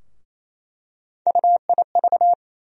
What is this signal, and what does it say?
Morse code, heard as a keyed tone, about 28 wpm, 700 Hz. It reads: UIV
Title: USA Call Sign Postfixes - 3 Letters 28wpm